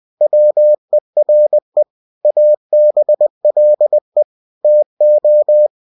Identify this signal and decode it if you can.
Morse code, heard as a keyed tone, about 20 wpm, 600 Hz. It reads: WERE ABLE TO
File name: cw-books / winnie-the-pooh / winnie-20wpm-page-0034